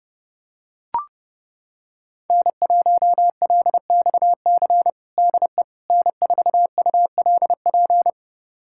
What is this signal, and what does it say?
Morse code, heard as a keyed tone, about 30 wpm, 700 Hz. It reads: N1LXC DE N4ULP
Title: USA Callsign DE Callsign - 5 Characters - 30wpm